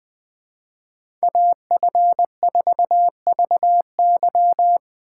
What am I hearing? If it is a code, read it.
Morse code, heard as a keyed tone, about 20 wpm, 700 Hz. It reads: AF4VY
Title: Speed Racing - USA 2x2 Call Signs 17wpm